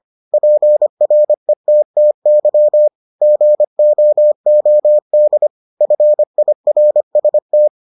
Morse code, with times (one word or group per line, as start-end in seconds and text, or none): 0.3-2.9 PRETTY
3.2-5.5 GOOD
5.8-7.7 FIRST